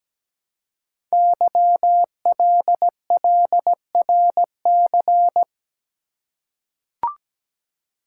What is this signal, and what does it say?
Morse code, heard as a keyed tone, about 17 wpm, 700 Hz. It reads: YLLRC
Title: Code Groups - 5 Letters - 17wpm